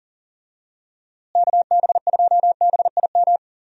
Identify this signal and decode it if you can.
Morse code, heard as a keyed tone, about 40 wpm, 700 Hz. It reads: KB2BIM
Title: USA Call Signs - 6 Characters - Repeated 2x - 40wpm